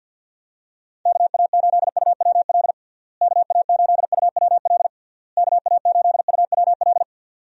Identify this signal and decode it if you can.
Morse code, heard as a keyed tone, about 50 wpm, 700 Hz. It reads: KA8UWL KA8UWL KA8UWL
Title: USA Call Signs - 6 Characters - Repeated 3x - 50wpm